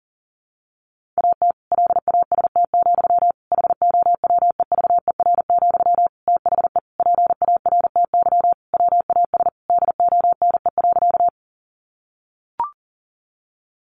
Morse code, heard as a keyed tone, about 40 wpm, 700 Hz. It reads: AT LAST, HOWEVER, THE PARTY WAS DONE.
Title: Sentences from Top 1400 Words 40wpm